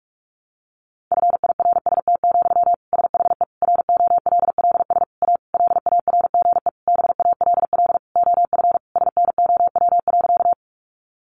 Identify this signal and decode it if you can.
Morse code, heard as a keyed tone, about 44 wpm, 700 Hz. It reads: FIRST, SHE ROLLS A LARGE BALL OF SNOW.
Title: Easy Phrases 45wpm